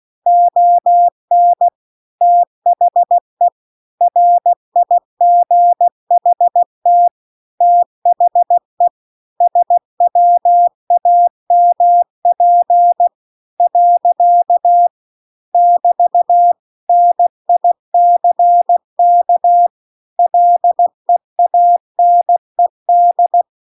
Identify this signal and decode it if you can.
Morse code, heard as a keyed tone, about 16 wpm, 700 Hz. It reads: ON THE RIGHT THE SWAMP . <BT> NICK LEANED
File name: CW-The-Big-Twoheaded-River-2-16WPM